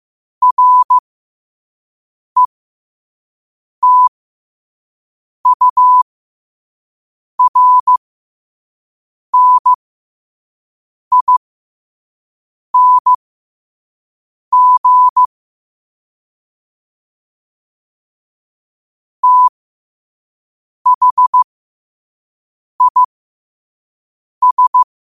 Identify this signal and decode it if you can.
Morse code, heard as a keyed tone, about 15 wpm, spaced as for 5 wpm, 1000 Hz. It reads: RETURNING THIS